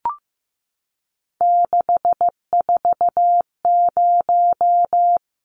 Morse code, heard as a keyed tone, about 15 wpm, 700 Hz. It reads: 640
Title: Numbers - 3-Digit 15wpm